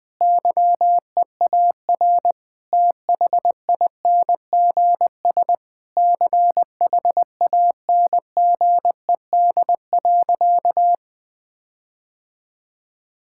Morse code, heard as a keyed tone, about 21 wpm, 700 Hz. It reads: YEAR THINGS CHANGED.
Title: Sentences with 5-Word Phrases - Spoken Beginning - 20wpm